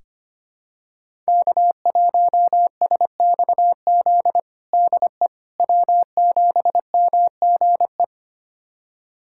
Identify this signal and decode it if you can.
Morse code, heard as a keyed tone, about 25 wpm, 700 Hz. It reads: K1SXZ DE W7MGE
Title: USA Callsign DE Callsign - 5 Characters - 25wpm